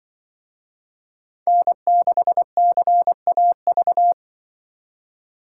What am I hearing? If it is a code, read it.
Morse code, heard as a keyed tone, about 25 wpm, 700 Hz. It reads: N6CAV
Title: Speed Racing - USA 1x3 Call Signs 20wpm